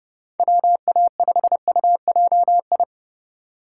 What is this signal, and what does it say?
Morse code, heard as a keyed tone, about 30 wpm, 700 Hz. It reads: WA5UJI
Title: USA Call Signs - 6 Characters - 3x Character Spacing - 30wpm